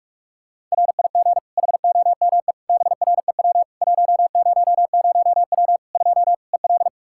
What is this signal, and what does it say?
Morse code, heard as a keyed tone, about 46 wpm, 700 Hz. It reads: RIG HOME BREW 100W 2 EL
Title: Practice QSO - Exchange 2 - Info - 45wpm